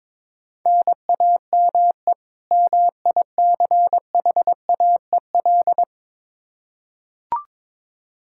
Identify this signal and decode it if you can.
Morse code, heard as a keyed tone, about 22 wpm, 700 Hz. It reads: NAME MICHAEL
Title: Practice QSO - Exchange 1 - Beginning - 22wpm